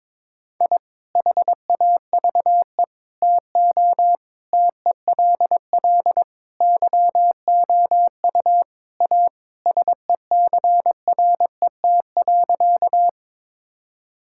Morse code, heard as a keyed tone, about 22 wpm, 700 Hz. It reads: I HAVE TO TELL YOU A SECRET.
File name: practice / 22wpm / Sentences with 3-Word Phrases - 22wpm